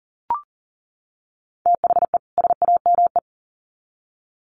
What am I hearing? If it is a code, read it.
Morse code, heard as a keyed tone, about 40 wpm, 700 Hz. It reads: THE SAME
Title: Sets of 2 Words 40wpm